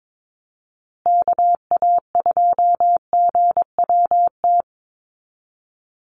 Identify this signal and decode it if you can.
Morse code, heard as a keyed tone, about 22 wpm, 700 Hz. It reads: KA2GWT
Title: USA Call Signs Repeated 2x 22wpm